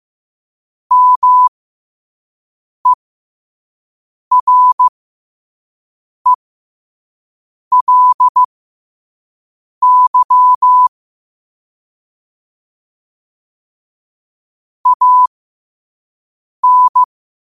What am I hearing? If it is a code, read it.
Morse code, heard as a keyed tone, about 15 wpm, spaced as for 5 wpm, 1000 Hz. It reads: MERELY AN
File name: qotdfiles / 5wpm